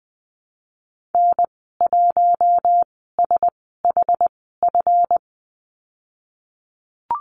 Morse code, heard as a keyed tone, about 20 wpm, spaced as for 15 wpm, 700 Hz. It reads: N1SHF
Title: 167 Lesson 27g - Number 7 - Call Signs 20wpm